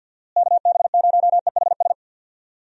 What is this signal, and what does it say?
Morse code, heard as a keyed tone, about 51 wpm, 700 Hz. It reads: KB0EHS
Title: USA Call Signs - 6 Characters - Repeated 2x - 50wpm